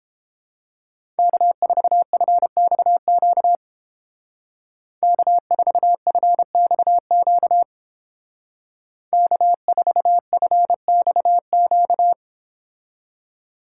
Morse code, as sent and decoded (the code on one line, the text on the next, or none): -.- ....- ..-. -..- --.- / -.- ....- ..-. -..- --.- / -.- ....- ..-. -..- --.-
K4FXQ K4FXQ K4FXQ